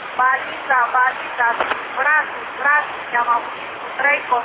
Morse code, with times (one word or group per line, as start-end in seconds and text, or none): none